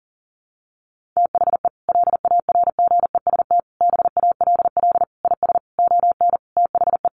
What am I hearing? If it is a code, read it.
Morse code, heard as a keyed tone, about 37 wpm, 700 Hz. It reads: THE LARGEST BALL IS ON THE
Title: Easy Phrases 40wpm